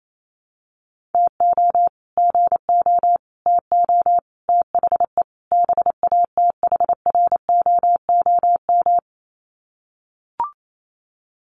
Morse code, heard as a keyed tone, about 28 wpm, 700 Hz. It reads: TO GO TO THE BATHROOM
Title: Sets of 5 Words Encore 28wpm